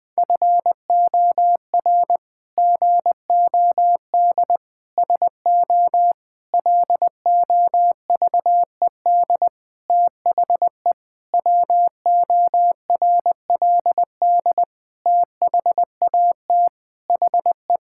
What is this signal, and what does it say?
Morse code, heard as a keyed tone, about 19 wpm, 700 Hz. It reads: FOR GOD SO LOVED THE WORLD THAT HE
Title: Sentences with 8-Word Phrases - Spoken Beginning - 20wpm